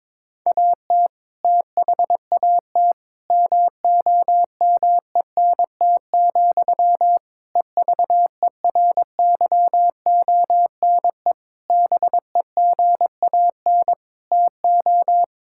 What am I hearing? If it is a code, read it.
Morse code, heard as a keyed tone, about 21 wpm, 700 Hz. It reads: AT THAT MOMENT, EVERYONE BEGAN TO
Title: Sentences from Top 1000 Words - Review Part 5 - 22wpm